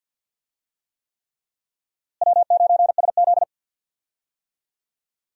Morse code, heard as a keyed tone, about 52 wpm, 700 Hz. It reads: W9SZ
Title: USA Call Signs - 4 Characters - Repeated 3x - 50wpm